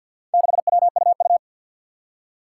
Morse code, heard as a keyed tone, about 51 wpm, 700 Hz. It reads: 6WUU
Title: Four Challenging Letter-Number Combinations 50wpm